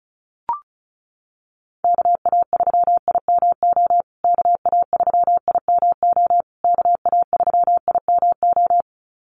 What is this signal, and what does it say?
Morse code, heard as a keyed tone, about 34 wpm, 700 Hz. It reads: KA3IMO KA3IMO KA3IMO
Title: USA Call Signs Repeated 3x 35wpm